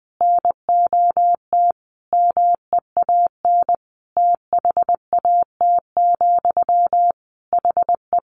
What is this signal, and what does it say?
Morse code, heard as a keyed tone, about 20 wpm, 700 Hz. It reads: NOT MEAN THAT, HE
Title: Sentences from Top 600 Words 20wpm